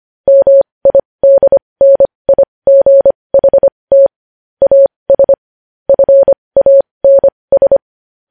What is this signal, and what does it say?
Morse code, heard as a keyed tone, about 25 wpm, spaced as for 20 wpm, 550 Hz. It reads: MIDNIGHT AS FANS
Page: news-headlines-in-morse-code-at-20-wpm